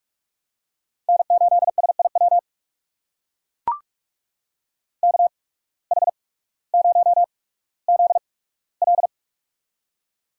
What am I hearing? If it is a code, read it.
Morse code, heard as a keyed tone, about 44 wpm, spaced as for 19 wpm, 700 Hz. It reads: N8SIW KH0ZL
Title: USA Call Signs - 5 Characters - 8x Character Spacing - 45wpm